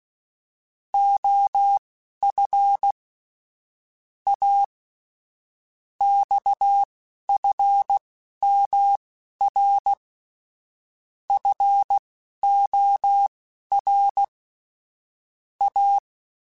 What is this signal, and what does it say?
Morse code, heard as a keyed tone, about 16 wpm, spaced as for 11 wpm, 800 Hz. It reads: OF A XFMR FOR A
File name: Technical Subjects / Technical Subjects-mp3s / Sw Pwr Sup @10